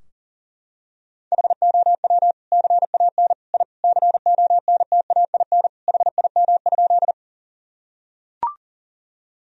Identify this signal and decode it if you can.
Morse code, heard as a keyed tone, about 40 wpm, 700 Hz. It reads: HOW CAN I CONTAIN HIM?
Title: Sentences from Top 1000 Words - Review Part 2 - 40wpm